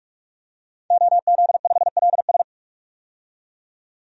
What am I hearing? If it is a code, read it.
Morse code, heard as a keyed tone, about 45 wpm, 700 Hz. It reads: OZ5LS
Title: DXCC-100 Call Signs - 5 Characters - 45wpm